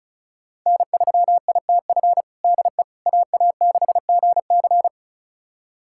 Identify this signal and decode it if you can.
Morse code, heard as a keyed tone, about 36 wpm, 690 Hz. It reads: N3ITF DE AA6GC
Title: USA Callsign DE Callsign - 5 Characters - 35wpm